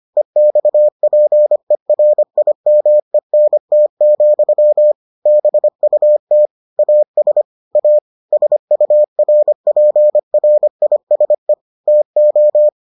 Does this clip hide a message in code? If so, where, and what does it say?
Yes: Morse code from 0.2 to 12.7 s, heard as a keyed tone, about 25 wpm, 600 Hz. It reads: EXPERIMENT, BUT AS A SURPRISE TO